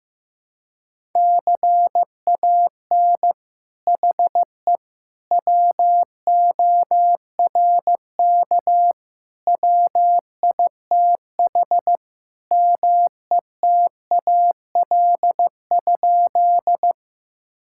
Morse code, heard as a keyed tone, about 15 wpm, 700 Hz. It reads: CAN HE WORK WITH METAL?
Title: Sentences from Top 1000 Words - Review Part 6 - 15wpm